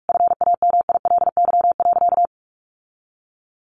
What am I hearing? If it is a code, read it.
Morse code, heard as a keyed tone, about 44 wpm, 700 Hz. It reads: FAMILY.
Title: Sentences from Top 1500 Words 45wpm